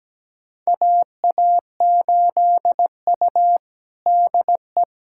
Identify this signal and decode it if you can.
Morse code, heard as a keyed tone, about 17 wpm, 700 Hz. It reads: AA8U DE
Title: USA Callsign DE Callsign 17wpm